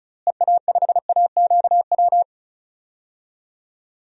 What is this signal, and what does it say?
Morse code, heard as a keyed tone, about 36 wpm, 700 Hz. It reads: EA5AQW
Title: DXCC-100 Call Signs - 6 Characters - Repeated 3x - 35wpm